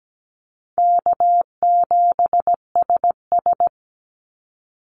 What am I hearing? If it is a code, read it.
Morse code, heard as a keyed tone, about 17 wpm, 700 Hz. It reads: K7SS
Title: Call Signs - 17wpm